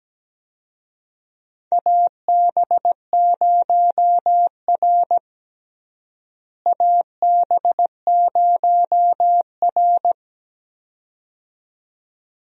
Morse code, heard as a keyed tone, about 17 wpm, 700 Hz. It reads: AB0R AB0R
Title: USA Call Signs - 4 Characters - Repeated 2x - 17wpm